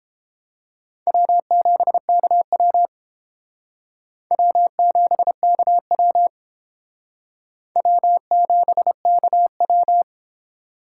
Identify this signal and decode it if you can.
Morse code, heard as a keyed tone, about 31 wpm, 700 Hz. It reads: W7KW W7KW W7KW